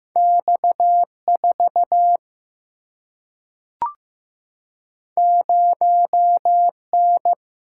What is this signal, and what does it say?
Morse code, heard as a keyed tone, about 16 wpm, 700 Hz. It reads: X4 0N
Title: Single-Letter Single-Number 15wpm